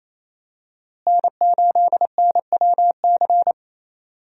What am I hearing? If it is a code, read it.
Morse code, heard as a keyed tone, about 28 wpm, 700 Hz. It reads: N8NWC
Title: USA Call Signs - 5 Characters - Repeated 3x - 28wpm